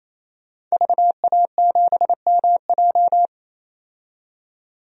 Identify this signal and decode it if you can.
Morse code, heard as a keyed tone, about 28 wpm, 700 Hz. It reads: VA7MJ
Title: DXCC-100 Call Signs - 5 Characters - 28wpm